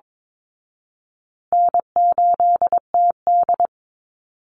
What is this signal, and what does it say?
Morse code, heard as a keyed tone, about 22 wpm, 690 Hz. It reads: N8TD
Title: Call Signs Repeated 3x - 22wpm